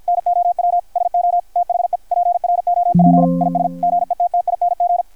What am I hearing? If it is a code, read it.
Morse code, heard as a keyed tone, about 53 wpm, 700 Hz. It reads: NOW SO THE PROP IS GETTING
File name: archived / CRSnet